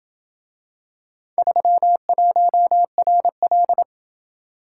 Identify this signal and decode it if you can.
Morse code, heard as a keyed tone, about 27 wpm, 700 Hz. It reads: 31RL